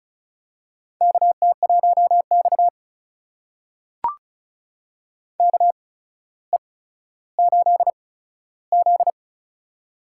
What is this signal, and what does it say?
Morse code, heard as a keyed tone, about 36 wpm, spaced as for 15 wpm, 700 Hz. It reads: KT1X KE8Z
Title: USA Call Signs - 4 Characters - 8x Character Spacing - 35wpm